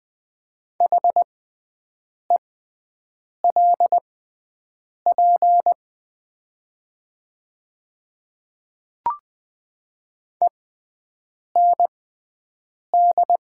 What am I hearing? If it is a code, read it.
Morse code, heard as a keyed tone, about 20 wpm, spaced as for 7 wpm, 700 Hz. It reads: HELP END
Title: 6x Character Spacing - Top 200 Words - 20wpm